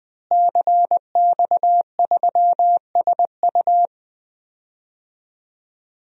Morse code, heard as a keyed tone, about 21 wpm, 700 Hz. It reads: CX3SU
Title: DXCC-100 Call Signs - 5 Characters - Repeated 3x - 20wpm